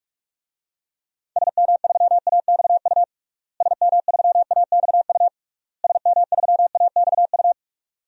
Morse code, heard as a keyed tone, about 43 wpm, 700 Hz. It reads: SM3AXU SM3AXU SM3AXU
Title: DXCC-100 Call Signs - 6 Characters - Repeated 3x - 45wpm